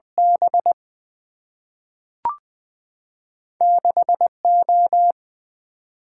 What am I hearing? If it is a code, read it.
Morse code, heard as a keyed tone, about 20 wpm, 700 Hz. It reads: B 6 O
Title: Single-Letter Single-Number 20wpm